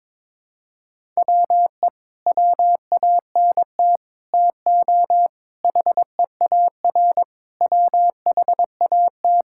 Morse code, heard as a keyed tone, about 22 wpm, 700 Hz. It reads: WE WANT TO HEAR WHAT
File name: practice / 22wpm / Sets of 6 Words - 22wpm